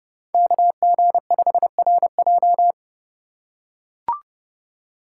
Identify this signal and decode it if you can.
Morse code, heard as a keyed tone, about 29 wpm, 700 Hz. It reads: KG5RJ